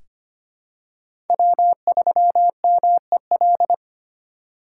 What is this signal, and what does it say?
Morse code, heard as a keyed tone, about 25 wpm, 700 Hz. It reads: W3MEL
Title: K1USN SST QSO Practice - 25wpm